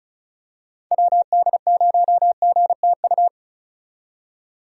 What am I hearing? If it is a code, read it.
Morse code, heard as a keyed tone, about 34 wpm, 700 Hz. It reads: WD0GTU
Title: USA Call Signs - 6 Characters - 35wpm